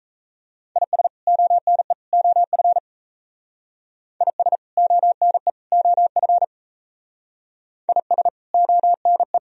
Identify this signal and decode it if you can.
Morse code, heard as a keyed tone, about 34 wpm, 700 Hz. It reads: IS ONE OF IS ONE OF IS ONE
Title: Speed-Racing - Sets of 3 Words 28wpm